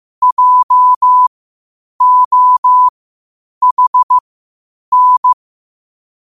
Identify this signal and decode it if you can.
Morse code, heard as a keyed tone, about 15 wpm, spaced as for 8 wpm, 1000 Hz. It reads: JOHN